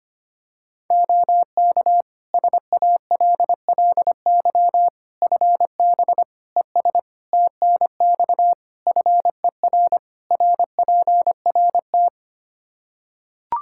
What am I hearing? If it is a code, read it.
Morse code, heard as a keyed tone, about 25 wpm, 700 Hz. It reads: OK SALLY FB ES TNX FER RPRT
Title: Practice QSO - Exchange 2 - Info - 25wpm